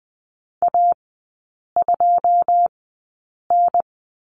Morse code, heard as a keyed tone, about 20 wpm, spaced as for 8 wpm, 690 Hz. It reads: A2N
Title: 081 Lesson 14e - Number 2 - Three Characters 20wpm